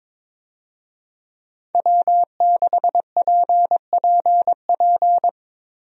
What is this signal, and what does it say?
Morse code, heard as a keyed tone, about 22 wpm, 700 Hz. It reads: W6PPP
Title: USA Call Signs - 5 Characters - Repeated 2x - 22wpm